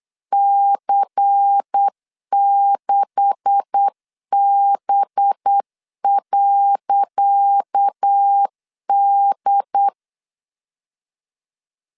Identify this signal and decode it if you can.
Morse code, heard as a keyed tone, about 8 wpm, 800 Hz. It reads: C6B.D
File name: LEZIONI / 10 SIGNS / 10C DOT 8wpm 3649